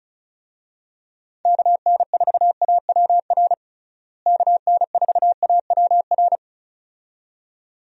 Morse code, heard as a keyed tone, about 36 wpm, 700 Hz. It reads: KN4AWR KN4AWR
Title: USA Call Signs - 6 Characters - Repeated 2x - 35wpm